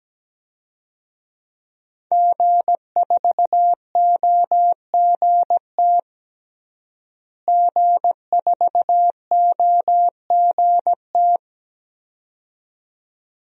Morse code, heard as a keyed tone, about 17 wpm, 700 Hz. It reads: G4OGT G4OGT
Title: DXCC-100 Call Signs - 5 Characters - Repeated 2x - 17wpm